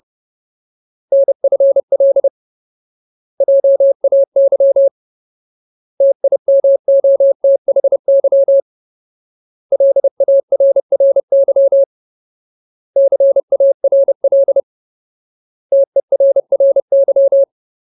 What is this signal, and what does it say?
Morse code, heard as a keyed tone, about 30 wpm, 550 Hz. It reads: NFL JAY TIMOTHY LARRY CARL TERRY
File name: Ultimate Word List / 30WPM Ultimate